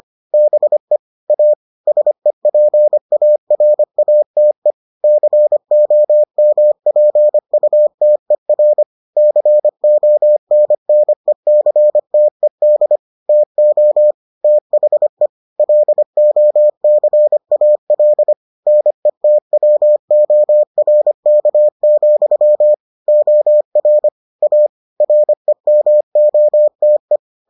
Morse code, heard as a keyed tone, about 25 wpm, 600 Hz. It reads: BE A SEPARATE COMPUTER CONNECTED TO THE LOCAL NETWORK, OR A REMOTE